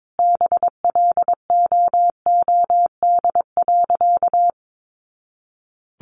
Morse code, heard as a keyed tone, about 22 wpm, 700 Hz. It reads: BLOOD.
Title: A Sound Of Thunder 22wpm